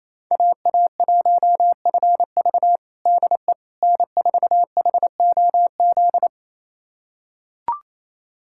Morse code, heard as a keyed tone, about 28 wpm, 700 Hz. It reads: AA1FV DE N4HOZ
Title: USA Callsign DE Callsign - 5 Characters - 28wpm